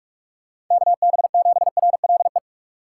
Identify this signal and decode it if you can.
Morse code, heard as a keyed tone, about 41 wpm, 700 Hz. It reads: KB7RLE
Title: USA Call Signs - 6 Characters - 4x Character Spacing - 45wpm